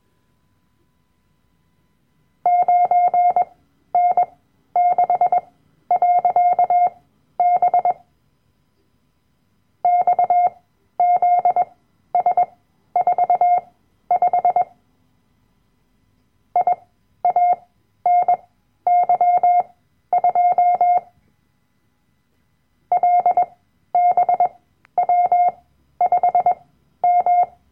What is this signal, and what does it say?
Morse code, heard as a keyed tone, about 21 wpm, spaced as for 12 wpm, 710 Hz. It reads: 9N6.B XZS45 IANY2 LBW5M
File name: Workshop Dateien / CW-Online-Kurs / Lektion 010